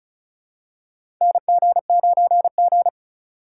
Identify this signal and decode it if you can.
Morse code, heard as a keyed tone, about 36 wpm, 700 Hz. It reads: NG9G
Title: USA Call Signs - 4 Characters - 6x Character Spacing - 35wpm